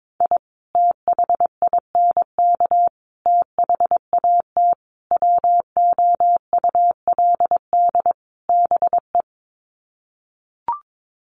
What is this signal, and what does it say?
Morse code, heard as a keyed tone, about 22 wpm, 700 Hz. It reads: I THINK THAT WOULD BE
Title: Sets of 5 Words Encore2 22wpm